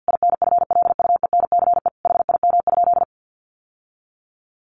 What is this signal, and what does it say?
Morse code, heard as a keyed tone, about 50 wpm, 700 Hz. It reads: INFLUENCE HIM?
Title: Sentences from Top 1100 Words 50wpm